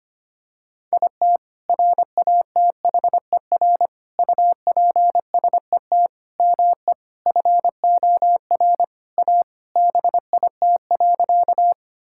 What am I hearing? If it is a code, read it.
Morse code, heard as a keyed tone, about 25 wpm, 700 Hz. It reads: IT RATHER UPSET ME FOR A BIT.
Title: Sentences from Top 1000 Words - Review Part 5 - 25wpm